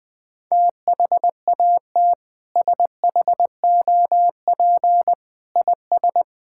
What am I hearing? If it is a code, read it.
Morse code, heard as a keyed tone, about 20 wpm, 700 Hz. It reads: THAT SHOP IS
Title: Sentences from Top 1600 Words 20wpm